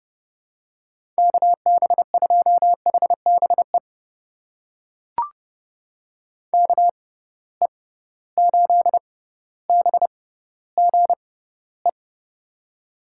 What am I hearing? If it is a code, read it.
Morse code, heard as a keyed tone, about 30 wpm, spaced as for 15 wpm, 700 Hz. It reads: KB2HBE KE8BGE
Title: USA Call Signs - 6 Characters - 6x Character Spacing - 30wpm